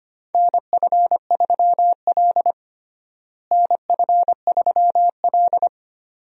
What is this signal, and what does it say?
Morse code, heard as a keyed tone, about 25 wpm, 700 Hz. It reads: NF3L NF3L
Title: USA Call Signs - 4 Characters - Repeated 2x - 25wpm